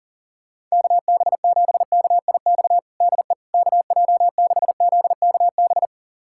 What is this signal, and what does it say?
Morse code, heard as a keyed tone, about 40 wpm, 700 Hz. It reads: KB7KIX DE KJ6ZKB
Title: USA Callsign DE Callsign 40wpm